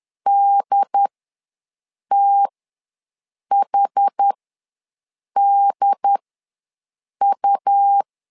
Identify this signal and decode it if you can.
Morse code, heard as a keyed tone, about 10 wpm, spaced as for 6 wpm, 800 Hz. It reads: DTHDU
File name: LEZIONI / 2 UHDN / 2 UHD ws 10wpm 2058